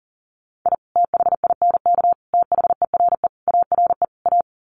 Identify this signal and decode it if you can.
Morse code, heard as a keyed tone, about 40 wpm, 700 Hz. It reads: I THINK THERE ARE A
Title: Sets of 5 Words Encore2 40wpm